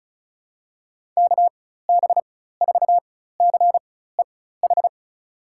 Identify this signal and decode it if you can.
Morse code, heard as a keyed tone, about 36 wpm, spaced as for 16 wpm, 700 Hz. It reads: KB4CEH
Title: USA Call Signs - 6 Characters - 4x Character Spacing - 35wpm